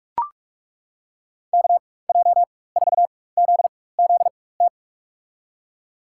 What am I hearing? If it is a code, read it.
Morse code, heard as a keyed tone, about 42 wpm, spaced as for 20 wpm, 700 Hz. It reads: KJ4ZZT
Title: USA Call Signs - 6 Characters - 4x Character Spacing - 45wpm